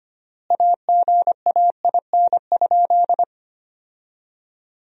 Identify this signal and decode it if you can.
Morse code, heard as a keyed tone, about 25 wpm, 700 Hz. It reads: AGAIN?
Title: 3x Word-Spacing - Sentences from Top 200 Words - 25wpm